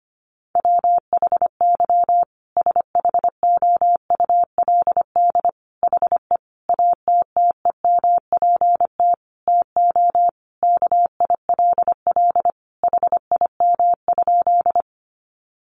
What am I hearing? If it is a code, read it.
Morse code, heard as a keyed tone, about 25 wpm, 700 Hz. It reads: WHY SHOULD HE ATTEMPT TO KILL HIM?
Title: Sentences from Top 1400 Words 25wpm